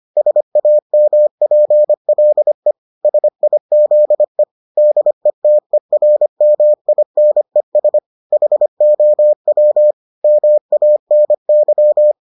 Morse code, heard as a keyed tone, about 25 wpm, 600 Hz. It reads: SAMPLE SIZE DETERMINES HOW MANY